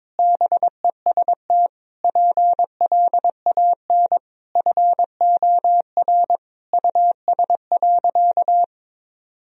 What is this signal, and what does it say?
Morse code, heard as a keyed tone, about 22 wpm, 700 Hz. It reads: BEST PLAN FOR US.
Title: Sentences from Top 1000 Words - Review Part 2 - 22wpm